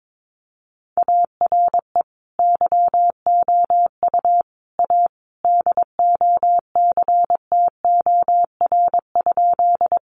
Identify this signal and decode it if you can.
Morse code, heard as a keyed tone, about 22 wpm, 700 Hz. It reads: ARE YOU A DOCTOR?
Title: Sentences from Top 1500 Words 22wpm